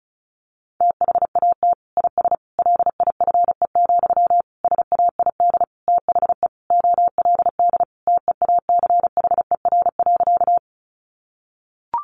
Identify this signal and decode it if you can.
Morse code, heard as a keyed tone, about 34 wpm, 700 Hz. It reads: THAT IS LIFE, SAID THE OLD TEACHER.
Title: Sentences from Top 900 Words 35wpm